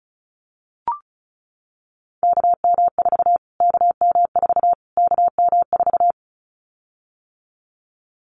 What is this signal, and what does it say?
Morse code, heard as a keyed tone, about 34 wpm, 700 Hz. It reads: KM4 KM4 KM4
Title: Callsign Prefixes for Top 25 DXCC - Repeated 3x - 35wpm